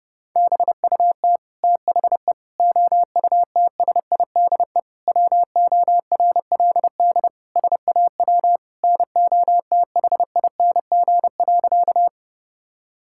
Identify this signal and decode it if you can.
Morse code, heard as a keyed tone, about 30 wpm, 700 Hz. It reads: BUT THE OUTSIDE WORLD SAW NOTHING.
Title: Sentences from Top 1000 Words - Review Part 6 - 30wpm